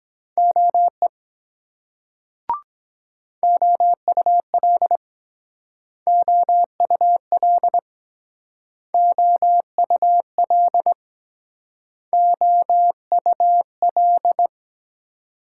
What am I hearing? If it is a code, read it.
Morse code, heard as a keyed tone, about 23 wpm, 700 Hz. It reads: OE OUL OUL OUL OUL